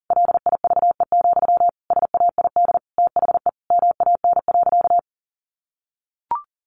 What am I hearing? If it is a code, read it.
Morse code, heard as a keyed tone, about 40 wpm, 700 Hz. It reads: LIVE, SAID THE MAN.
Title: Sentences from Top 200 Words 40wpm